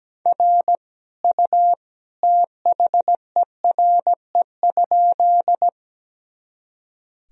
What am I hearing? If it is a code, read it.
Morse code, heard as a keyed tone, about 17 wpm, 700 Hz. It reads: R U THERE?
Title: Sentences from Top 100 Words and Abbreviations 17wpm